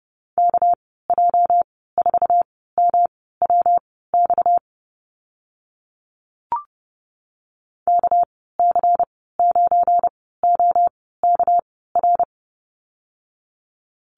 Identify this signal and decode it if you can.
Morse code, heard as a keyed tone, about 30 wpm, spaced as for 17 wpm, 700 Hz. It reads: KJ4MWX KC9OKR
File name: Lesson 37g - Letter x - Call Signs 30wpm